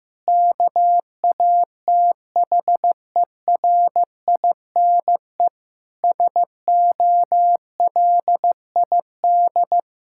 Morse code, heard as a keyed tone, about 15 wpm, 700 Hz. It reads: KATHERINE SOLID
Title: Practice QSO - Exchange 3 - More Info - 15wpm